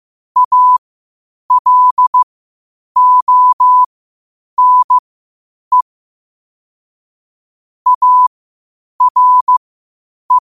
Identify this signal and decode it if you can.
Morse code, heard as a keyed tone, about 15 wpm, spaced as for 8 wpm, 1000 Hz. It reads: ALONE ARE
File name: qotdfiles / 8wpm